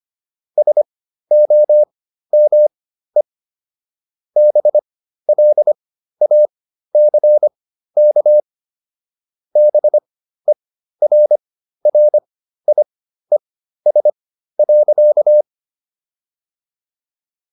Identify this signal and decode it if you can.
Morse code, heard as a keyed tone, about 25 wpm, spaced as for 13 wpm, 600 Hz. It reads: SOME BLACK BERRIES.